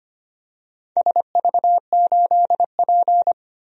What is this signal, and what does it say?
Morse code, heard as a keyed tone, about 25 wpm, 700 Hz. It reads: SV8P
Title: DXCC-100 Call Signs - 4 Characters - 25wpm